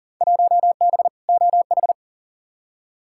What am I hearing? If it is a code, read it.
Morse code, heard as a keyed tone, about 40 wpm, 700 Hz. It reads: 1B OH
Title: Field Day Exchanges - 40wpm